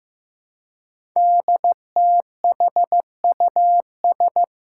Morse code, heard as a keyed tone, about 15 wpm, 700 Hz. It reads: DTHUS
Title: Five Challenging Letter-Number Combinations 15wpm